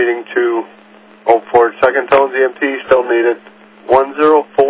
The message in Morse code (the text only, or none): none